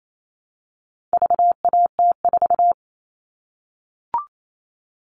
Morse code, heard as a keyed tone, about 28 wpm, 700 Hz. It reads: VAT4
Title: All Cut Numbers - 4-Digit 28wpm